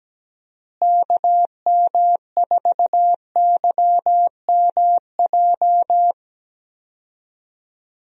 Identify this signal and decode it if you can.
Morse code, heard as a keyed tone, about 17 wpm, 700 Hz. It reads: KM4YMJ
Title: USA Call Signs - 6 Characters - Repeated 2x - 17wpm